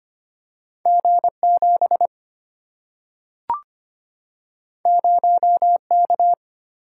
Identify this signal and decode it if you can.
Morse code, heard as a keyed tone, about 25 wpm, 700 Hz. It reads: G 7 0 K
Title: Single-Letter Single-Number 25wpm